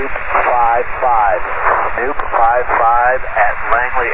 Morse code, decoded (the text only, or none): none